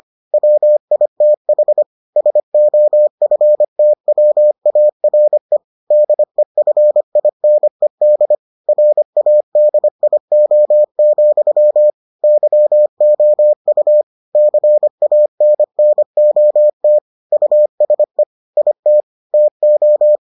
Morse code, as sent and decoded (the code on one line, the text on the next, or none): .-- .. - .... / ... --- ..-. - .-- .- .-. / -.. ..-. .. -. -.. / .-. .- -.. .. --- --..-- / -.-- --- ..- / -.-. .- -. -. --- - / ..- ... / .. - / - ---
WITH SOFTWARE DEFINED RADIO, YOU CANNOT USE IT TO